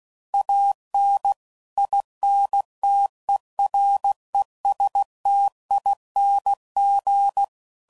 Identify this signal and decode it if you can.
Morse code, heard as a keyed tone, about 16 wpm, 800 Hz. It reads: AN INTERESTING
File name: CW Code Groups @15WPM/Technical Subjects